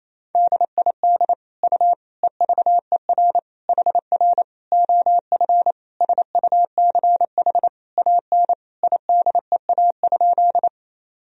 Morse code, heard as a keyed tone, about 28 wpm, 700 Hz. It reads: DID U EVER HR OF SUCH AN IDEA?
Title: Sentences from Top 400 Words and Abbreviations 28wpm